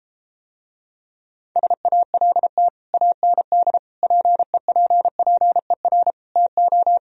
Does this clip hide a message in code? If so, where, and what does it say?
Yes: Morse code from 1.6 to 7.0 s, heard as a keyed tone, about 33 wpm, 700 Hz. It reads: SALT AND PEPPER TO